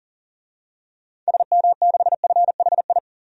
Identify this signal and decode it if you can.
Morse code, heard as a keyed tone, about 40 wpm, 700 Hz. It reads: SM6FHI
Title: DXCC-100 Call Signs - 6 Characters - 40wpm